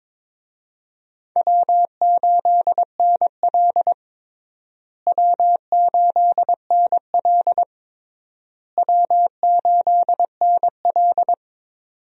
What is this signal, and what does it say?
Morse code, heard as a keyed tone, about 22 wpm, 700 Hz. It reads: W8NL W8NL W8NL
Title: USA Call Signs - 4 Characters - Repeated 3x - 22wpm